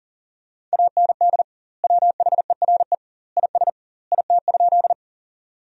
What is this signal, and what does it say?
Morse code, heard as a keyed tone, about 40 wpm, 700 Hz. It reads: AND WHERE IS IT?
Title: 2x Word-Spacing - Sentences from Top 200 Words - 40wpm